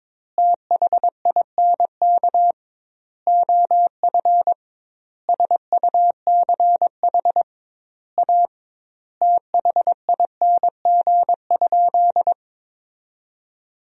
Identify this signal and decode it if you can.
Morse code, heard as a keyed tone, about 22 wpm, 700 Hz. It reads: THINK OF SUCH A THING?
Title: 2x Word-Spacing - Sentences from Top 200 Words - 22wpm